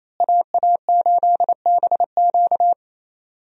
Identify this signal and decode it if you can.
Morse code, heard as a keyed tone, about 28 wpm, 700 Hz. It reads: AA8BQ
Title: USA Call Signs - 5 Characters - Repeated 3x - 28wpm